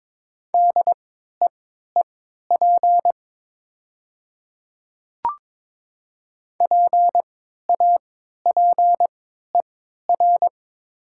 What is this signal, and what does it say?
Morse code, heard as a keyed tone, about 22 wpm, spaced as for 12 wpm, 700 Hz. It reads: DEEP PAPER
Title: 3x Character Spacing - Top 500 Words - 22wpm